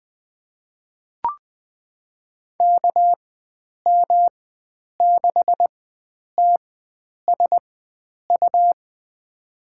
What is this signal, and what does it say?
Morse code, heard as a keyed tone, about 20 wpm, spaced as for 9 wpm, 700 Hz. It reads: KM6TSU
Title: USA Call Signs - 6 Characters - 4x Character Spacing - 20wpm